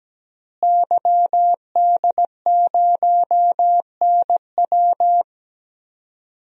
Morse code, heard as a keyed tone, about 17 wpm, 700 Hz. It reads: YD0NW
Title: DXCC-100 Call Signs - 5 Characters - Repeated 3x - 17wpm